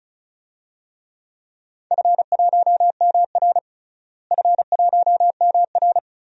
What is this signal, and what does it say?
Morse code, heard as a keyed tone, about 36 wpm, 700 Hz. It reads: F1MR F1MR
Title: DXCC-100 Call Signs - 4 Characters - Repeated 2x - 35wpm